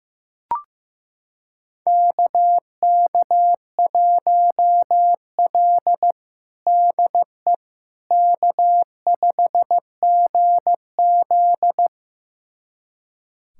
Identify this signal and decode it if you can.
Morse code, heard as a keyed tone, about 16 wpm, 700 Hz. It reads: KK1L DE K5GZ
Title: USA Callsign DE Callsign - 4 Characters - 15wpm